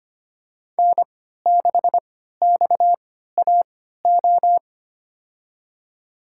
Morse code, heard as a keyed tone, about 25 wpm, spaced as for 14 wpm, 700 Hz. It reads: N6XAO
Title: USA Call Signs - 5 Characters - 3x Character Spacing - 25wpm